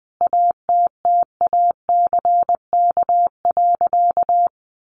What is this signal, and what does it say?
Morse code, heard as a keyed tone, about 20 wpm, 700 Hz. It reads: ATTACK.
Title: Sentences from Top 1500 Words 20wpm